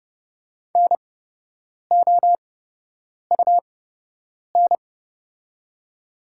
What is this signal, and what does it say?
Morse code, heard as a keyed tone, about 30 wpm, spaced as for 8 wpm, 700 Hz. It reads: NOUN
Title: 8x Character Spacing - Top 500 Words - 30wpm